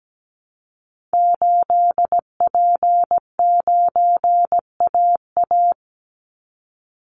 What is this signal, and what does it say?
Morse code, heard as a keyed tone, about 17 wpm, 700 Hz. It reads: 8P9AA
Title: Call Signs Repeated 3x - 17wpm